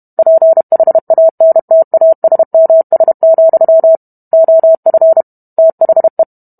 Morse code, heard as a keyed tone, about 32 wpm, 650 Hz. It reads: PHANTASMS, OF THE